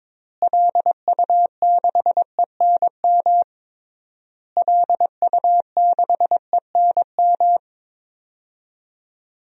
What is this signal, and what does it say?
Morse code, heard as a keyed tone, about 22 wpm, 700 Hz. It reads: LU6ENM LU6ENM
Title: DXCC-100 Call Signs - 6 Characters - Repeated 3x - 22wpm